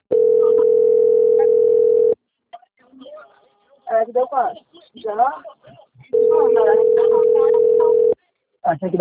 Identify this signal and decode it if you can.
Ringback tone (North America), 440 + 480 Hz, with speech.
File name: out- - - - - .5040